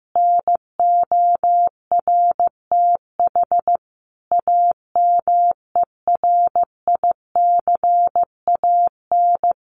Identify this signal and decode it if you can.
Morse code, heard as a keyed tone, about 15 wpm, 700 Hz. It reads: NORTH AMERICAN
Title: Sets of 5 Words Encore 15wpm